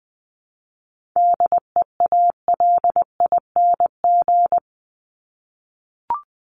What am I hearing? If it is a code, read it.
Morse code, heard as a keyed tone, about 20 wpm, 700 Hz. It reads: DEALING
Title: ING Suffix - Top 100 Words with Base Words 20wpm